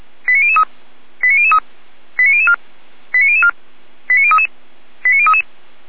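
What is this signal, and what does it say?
Five-tone selective call (ZVEI), heard as quick runs of five tones, tones of about 70 ms. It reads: 89003 89003 89004 89004 89033 89033
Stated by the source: Five-tone ZVEI selective call received on SDR, from the Feuerwehr Orlamünde (fire brigade) audio archive.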